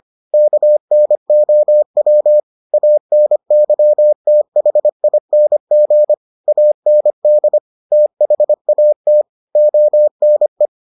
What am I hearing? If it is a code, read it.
Morse code, heard as a keyed tone, about 25 wpm, 600 Hz. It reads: KNOW ANYTHING AND THAT ONE